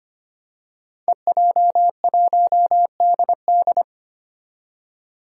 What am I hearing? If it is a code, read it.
Morse code, heard as a keyed tone, about 25 wpm, 700 Hz. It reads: EJ1DD
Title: Five Challenging Letter-Number Combinations 25wpm